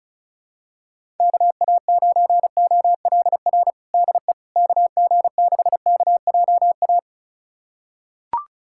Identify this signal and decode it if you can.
Morse code, heard as a keyed tone, about 36 wpm, 700 Hz. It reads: KA9OLR DE KG6KJA
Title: USA Callsign DE Callsign - 6 Characters - 35wpm